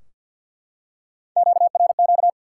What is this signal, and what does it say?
Morse code, heard as a keyed tone, about 52 wpm, 700 Hz. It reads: QRQ
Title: Top 90 QSO Elements 50wpm